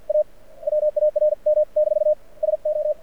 Morse code, heard as a keyed tone, about 49 wpm, spaced as for 43 wpm, 600 Hz. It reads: A WARM- SO
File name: archived / CRSnet